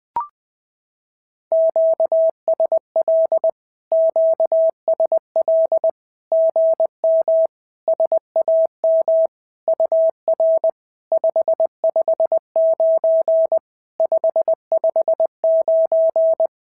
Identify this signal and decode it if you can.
Morse code, heard as a keyed tone, about 20 wpm, 650 Hz. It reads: QSL QSL GM SAM UR 559 559
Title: SOTA QSO Practice 20wpm